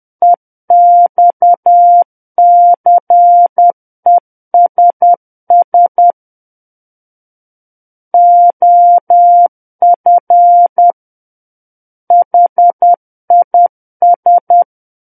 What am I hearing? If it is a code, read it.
Morse code, heard as a keyed tone, about 10 wpm, 700 Hz. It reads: EXCESS OF HIS
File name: Book One / 01-Chapter 1 - 10 WPM